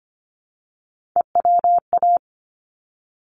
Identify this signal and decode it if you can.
Morse code, heard as a keyed tone, about 25 wpm, 700 Hz. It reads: EWA
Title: ARRL Sections 25wpm